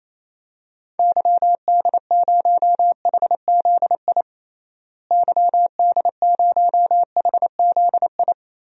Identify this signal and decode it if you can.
Morse code, heard as a keyed tone, about 28 wpm, 700 Hz. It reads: YD0HZI YD0HZI
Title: DXCC-100 Call Signs - 6 Characters - Repeated 3x - 28wpm